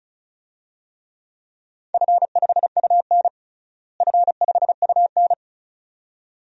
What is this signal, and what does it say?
Morse code, heard as a keyed tone, about 36 wpm, 700 Hz. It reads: F5UN F5UN
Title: DXCC-100 Call Signs - 4 Characters - Repeated 2x - 35wpm